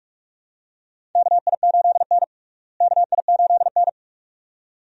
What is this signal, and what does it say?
Morse code, heard as a keyed tone, about 45 wpm, 700 Hz. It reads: KI8N KI8N
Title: USA Call Signs - 4 Characters - Repeated 2x - 45wpm